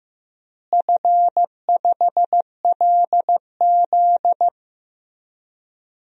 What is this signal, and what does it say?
Morse code, heard as a keyed tone, about 15 wpm, 700 Hz. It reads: F5LZ
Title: DXCC-100 Call Signs - 4 Characters - Repeated 2x - 15wpm